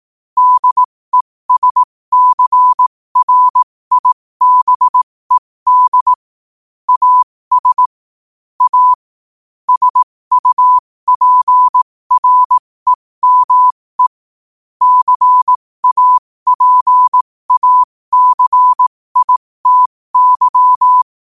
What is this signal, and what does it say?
Morse code, heard as a keyed tone, about 18 wpm, spaced as for 15 wpm, 1000 Hz. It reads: DESCRIBED AS A SUPREME CAPACITY
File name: qotdfiles / 15wpm